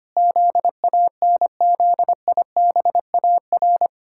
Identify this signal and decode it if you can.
Morse code, heard as a keyed tone, about 25 wpm, 700 Hz. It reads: ZANZIBAR
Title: Pangram Challenge v2 - 25wpm